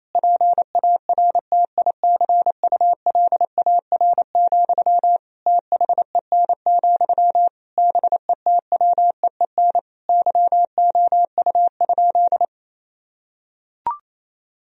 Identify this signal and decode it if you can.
Morse code, heard as a keyed tone, about 28 wpm, 700 Hz. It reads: PARTICULAR, THEN, BETWEEN YOU?
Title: Sentences from Top 1000 Words - Review Part 4 - 28wpm